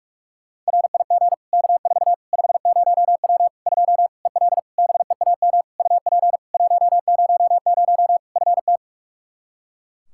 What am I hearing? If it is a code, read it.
Morse code, heard as a keyed tone, about 46 wpm, 700 Hz. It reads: RIG K4 50W 2 EL BEAM UP 100 FT
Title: Practice QSO - Exchange 2 - Info - 45wpm